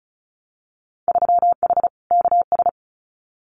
Morse code, heard as a keyed tone, about 36 wpm, 700 Hz. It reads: 3H KS
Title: Winter Field Day Exchanges Encore 35wpm